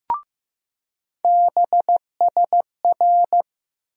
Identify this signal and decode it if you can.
Morse code, heard as a keyed tone, about 15 wpm, 700 Hz. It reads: BSR
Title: USA Call Sign Postfixes - 3 Letters 15wpm